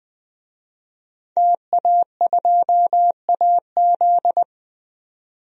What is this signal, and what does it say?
Morse code, heard as a keyed tone, about 20 wpm, 700 Hz. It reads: TA2AZ
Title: DXCC-100 Call Signs - 5 Characters - 20wpm